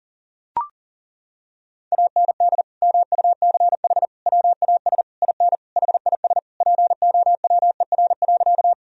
Morse code, heard as a keyed tone, about 37 wpm, 700 Hz. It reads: AND MUCH WAS IN HIS POWER.
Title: Sentences from Top 1000 Words - Review Part 2 - 40wpm